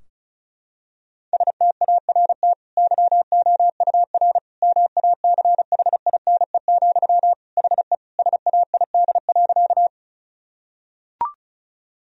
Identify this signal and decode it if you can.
Morse code, heard as a keyed tone, about 36 wpm, 700 Hz. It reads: START YOUR MACHINE, HE SAID.
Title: Sentences from Top 1000 Words - Review Part 2 - 35wpm